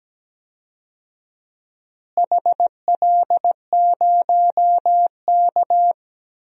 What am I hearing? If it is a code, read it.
Morse code, heard as a keyed tone, about 17 wpm, 700 Hz. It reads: HL0K
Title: DXCC-100 Call Signs - 4 Characters - Repeated 3x - 17wpm